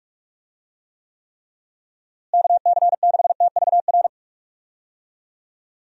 Morse code, heard as a keyed tone, about 46 wpm, 700 Hz. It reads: KC6TVR
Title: USA Call Signs - 6 Characters - Repeated 3x - 45wpm